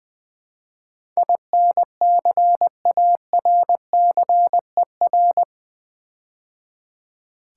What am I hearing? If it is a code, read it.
Morse code, heard as a keyed tone, about 20 wpm, 700 Hz. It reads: INCARCER
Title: Build A Word - 4 Letters At A Time - 11-Letter Words - 20wpm